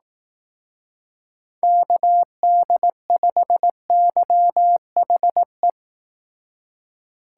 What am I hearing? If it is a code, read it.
Morse code, heard as a keyed tone, about 18 wpm, 700 Hz. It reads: KD5YHE
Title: Speed Racing - USA Call Signs 15wpm